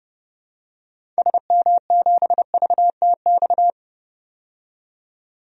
Morse code, heard as a keyed tone, about 30 wpm, 700 Hz. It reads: SM7VTX
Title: DXCC-100 Call Signs - 6 Characters - 30wpm